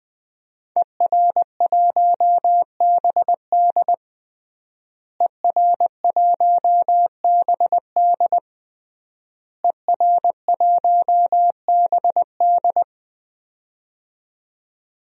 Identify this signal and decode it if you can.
Morse code, heard as a keyed tone, about 20 wpm, 700 Hz. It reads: ER1BD ER1BD ER1BD
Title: DXCC-100 Call Signs - 5 Characters - Repeated 3x - 20wpm